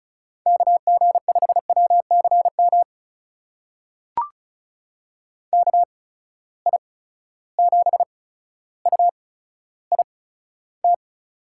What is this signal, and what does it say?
Morse code, heard as a keyed tone, about 36 wpm, spaced as for 15 wpm, 700 Hz. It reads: KG5WCM KI7UIT
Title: USA Call Signs - 6 Characters - 8x Character Spacing - 35wpm